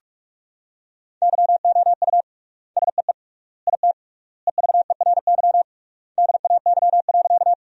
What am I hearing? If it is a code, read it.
Morse code, heard as a keyed tone, about 41 wpm, 700 Hz. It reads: YOU SEE IT EVERY DAY.
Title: 3x Word-Spacing - Sentences from Top 200 Words - 45wpm